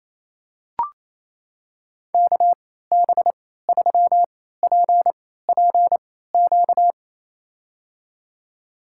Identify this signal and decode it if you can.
Morse code, heard as a keyed tone, about 28 wpm, spaced as for 16 wpm, 700 Hz. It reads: KB3PPQ
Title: USA Call Signs - 6 Characters - 3x Character Spacing - 28wpm